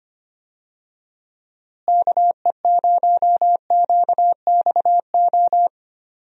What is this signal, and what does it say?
Morse code, heard as a keyed tone, about 25 wpm, 700 Hz. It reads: KE0QXO
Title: USA Call Signs - 6 Characters - Repeated 3x - 25wpm